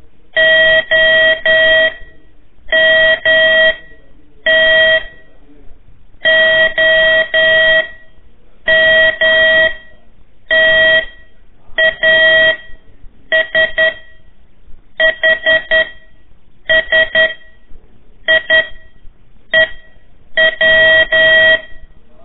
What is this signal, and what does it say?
Morse code, heard as a keyed tone, about 10 wpm, spaced as for 7 wpm, 1900 Hz. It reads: OMT OMTASHSIEW
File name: 3 lezione CW 1 AIR